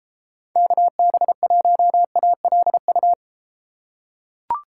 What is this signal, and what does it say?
Morse code, heard as a keyed tone, about 33 wpm, 700 Hz. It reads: KB1ALU